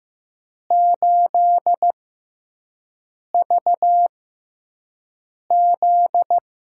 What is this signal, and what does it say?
Morse code, heard as a keyed tone, about 15 wpm, spaced as for 5 wpm, 700 Hz. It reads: 8VZ
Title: USA Call Signs - 4 Characters - 6x Character Spacing - 15wpm